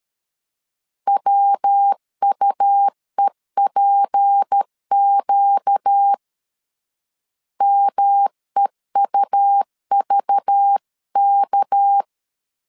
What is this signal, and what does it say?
Morse code, heard as a keyed tone, about 13 wpm, 800 Hz. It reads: WUEPQ MEUVK
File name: LEZIONI / 6 PKX0 / 6 PK 12wpm 2006